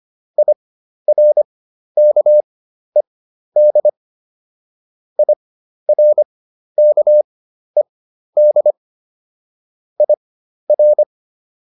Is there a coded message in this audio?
Yes: Morse code from 0.4 to 11.0 s, heard as a keyed tone, about 25 wpm, spaced as for 12 wpm, 600 Hz. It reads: IRKED IRKED IR